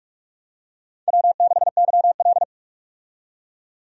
Morse code, heard as a keyed tone, about 43 wpm, 700 Hz. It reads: W6YL
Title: USA Call Signs - 4 Characters - 45wpm